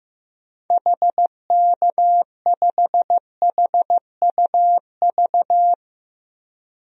Morse code, heard as a keyed tone, about 15 wpm, 700 Hz. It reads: HK5HUV
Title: DXCC-100 Call Signs - 6 Characters - 15wpm